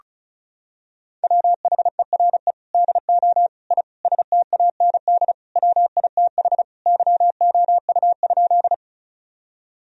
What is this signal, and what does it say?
Morse code, heard as a keyed tone, about 36 wpm, 700 Hz. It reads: WHERE DO I STAND WITH YOU?
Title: Sentences from Top 1000 Words - Review Part 1 - 35wpm